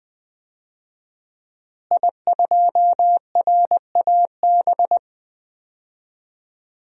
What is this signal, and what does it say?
Morse code, heard as a keyed tone, about 20 wpm, 700 Hz. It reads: I2RAB
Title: DXCC-100 Call Signs - 5 Characters - Repeated 3x - 20wpm